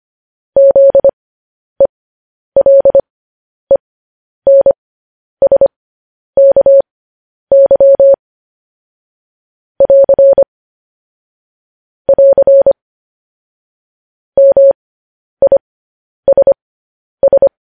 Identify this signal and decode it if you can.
Morse code, heard as a keyed tone, about 25 wpm, spaced as for 10 wpm, 550 Hz. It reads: ZELENSKY <AR> <AR> MISS